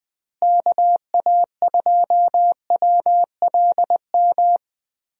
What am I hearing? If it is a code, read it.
Morse code, heard as a keyed tone, about 20 wpm, 700 Hz. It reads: KA2WLM
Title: USA Call Signs - 6 Characters - Repeated 3x - 20wpm